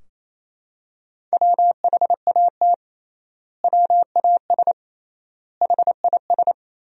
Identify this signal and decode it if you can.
Morse code, heard as a keyed tone, about 28 wpm, 700 Hz. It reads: WHAT WAS HIS
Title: 3x Word-Spacing - Sentences from Top 200 Words - 28wpm